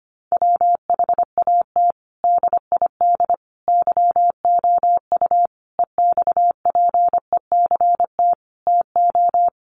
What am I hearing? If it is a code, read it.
Morse code, heard as a keyed tone, about 25 wpm, 700 Hz. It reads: WHAT DID YOU EXPECT TO
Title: Sentences from Top 1100 Words 25wpm